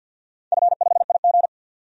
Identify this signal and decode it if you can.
Morse code, heard as a keyed tone, about 52 wpm, 700 Hz. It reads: F5IG
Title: DXCC-100 Call Signs - 4 Characters - 50wpm